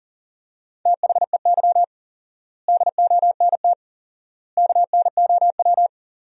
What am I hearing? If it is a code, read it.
Morse code, heard as a keyed tone, about 40 wpm, 700 Hz. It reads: THEY DONT KNOW
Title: 4x Word-Spacing - Sentences from Top 300 Words - 40wpm